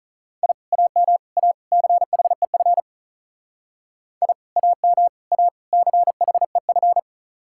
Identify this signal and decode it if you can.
Morse code, heard as a keyed tone, about 45 wpm, 700 Hz. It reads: I AM A CHEF I AM A CHEF